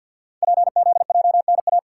Morse code, heard as a keyed tone, about 51 wpm, 700 Hz. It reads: PZJNA
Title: Code Groups - 5 Letters - 50wpm